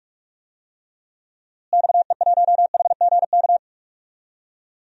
Morse code, heard as a keyed tone, about 46 wpm, 700 Hz. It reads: XE1HGK
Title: DXCC-100 Call Signs - 6 Characters - Repeated 3x - 45wpm